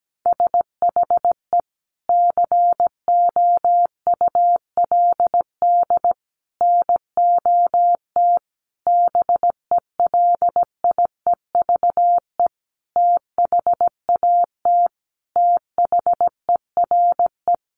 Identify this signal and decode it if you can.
Morse code, heard as a keyed tone, about 17 wpm, 700 Hz. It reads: SHE COULD NOT BELIEVE THAT THERE
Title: Sentences from Top 1000 Words 17wpm